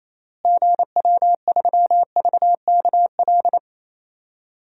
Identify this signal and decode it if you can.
Morse code, heard as a keyed tone, about 28 wpm, 700 Hz. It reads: GW3VKL
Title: DXCC-100 Call Signs - 6 Characters - Repeated 3x - 28wpm